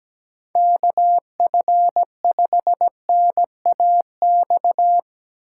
Morse code, heard as a keyed tone, about 17 wpm, 700 Hz. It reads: KF5NAX
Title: USA Call Signs - 6 Characters - 3x Character Spacing - 17wpm